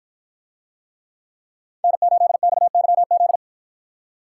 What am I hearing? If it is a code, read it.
Morse code, heard as a keyed tone, about 53 wpm, 700 Hz. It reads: N8XYZ